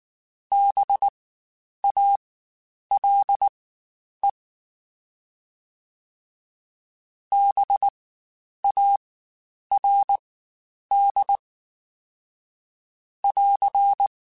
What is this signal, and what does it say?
Morse code, heard as a keyed tone, about 19 wpm, spaced as for 9 wpm, 800 Hz. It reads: BALE BARD <AR>
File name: Spaced Alphas / Spaced Alphas mp3s / Bs sp @18